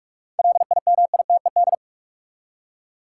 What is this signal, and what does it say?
Morse code, heard as a keyed tone, about 46 wpm, 700 Hz. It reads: LIMITED
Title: Seven Letter Words - 45wpm